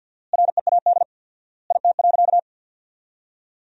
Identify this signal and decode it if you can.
Morse code, heard as a keyed tone, about 53 wpm, 700 Hz. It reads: READ IT.
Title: 4x Word-Spacing - Sentences from Top 200 Words - 50wpm